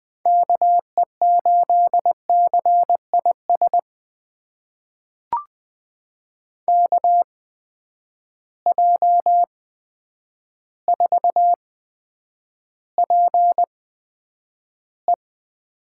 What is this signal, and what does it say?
Morse code, heard as a keyed tone, about 20 wpm, spaced as for 9 wpm, 700 Hz. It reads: KE8CIS KJ4PE
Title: USA Call Signs - 8x Character Spacing - 20wpm